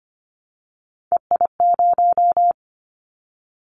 Morse code, heard as a keyed tone, about 25 wpm, 700 Hz. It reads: EI0
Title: Callsign Prefixes for Top 50 DXCC - 25wpm